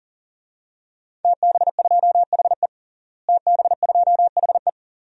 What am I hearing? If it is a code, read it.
Morse code, heard as a keyed tone, about 40 wpm, 700 Hz. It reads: TB2HE TB2HE
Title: DXCC-100 Call Signs - 5 Characters - Repeated 3x - 40wpm